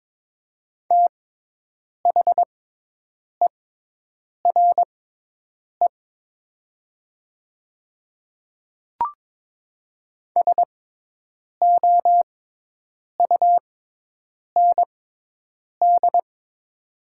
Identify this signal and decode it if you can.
Morse code, heard as a keyed tone, about 22 wpm, spaced as for 8 wpm, 700 Hz. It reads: THERE SOUND
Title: 6x Character Spacing - Top 100 Words - 22wpm